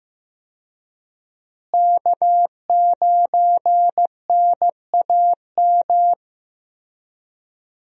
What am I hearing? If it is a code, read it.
Morse code, heard as a keyed tone, about 15 wpm, 700 Hz. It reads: K9NAM
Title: USA Call Signs - 5 Characters - Repeated 3x - 15wpm